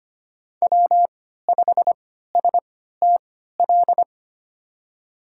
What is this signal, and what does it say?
Morse code, heard as a keyed tone, about 25 wpm, spaced as for 14 wpm, 700 Hz. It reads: W5STL
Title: USA Call Signs - 5 Characters - 3x Character Spacing - 25wpm